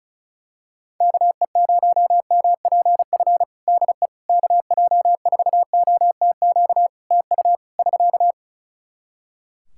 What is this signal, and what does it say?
Morse code, heard as a keyed tone, about 34 wpm, 700 Hz. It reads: KE0MPF DE KJ4OTQ TU <SK>
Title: Practice QSO - Exchange 4 - Ending - 35wpm